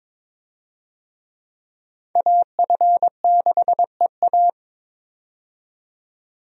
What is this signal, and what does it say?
Morse code, heard as a keyed tone, about 22 wpm, 700 Hz. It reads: AF6EA
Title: USA Call Signs - 5 Characters - 22wpm